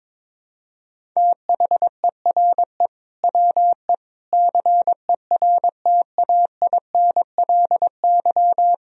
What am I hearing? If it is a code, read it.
Morse code, heard as a keyed tone, about 22 wpm, 700 Hz. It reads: THERE WE CERTAINLY
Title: Sentences from Top 1000 Words - Review Part 1 - 22wpm